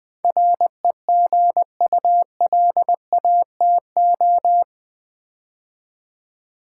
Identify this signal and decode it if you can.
Morse code, heard as a keyed tone, about 20 wpm, 700 Hz. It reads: REGULATO
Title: Build A Word - 4 Letters At A Time - 10-Letter Words - 20wpm